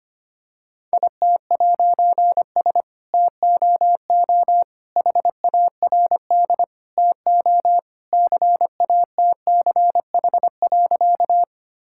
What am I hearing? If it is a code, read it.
Morse code, heard as a keyed tone, about 25 wpm, 700 Hz. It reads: IT'S TOO HARD TO CATCH.
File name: practice / 25wpm / All-Age Appropriate Jokes - 25wpm